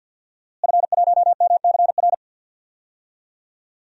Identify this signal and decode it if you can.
Morse code, heard as a keyed tone, about 52 wpm, 700 Hz. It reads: F1MCR
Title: DXCC-100 Call Signs - 5 Characters - Repeated 2x - 50wpm